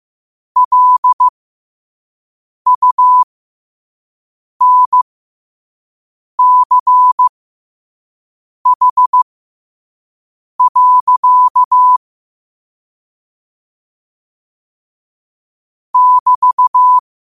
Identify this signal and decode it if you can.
Morse code, heard as a keyed tone, about 15 wpm, spaced as for 5 wpm, 1000 Hz. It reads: LUNCH. <BT>
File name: qotdfiles / 5wpm